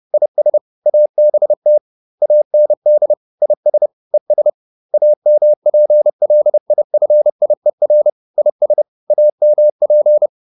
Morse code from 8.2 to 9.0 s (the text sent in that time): IS